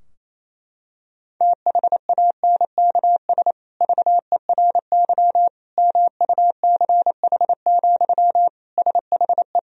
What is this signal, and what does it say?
Morse code, heard as a keyed tone, about 27 wpm, 700 Hz. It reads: THANKS VERY MUCH, SHE
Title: Sentences from Top 1000 Words - Review Part 3 - 28wpm